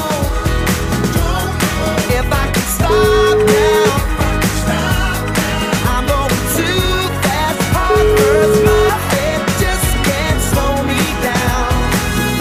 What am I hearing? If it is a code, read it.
Ringback tone (Europe, 425 Hz), with noise.